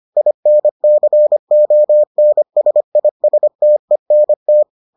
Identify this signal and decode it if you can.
Morse code, heard as a keyed tone, about 25 wpm, 600 Hz. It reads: INCONSISTENT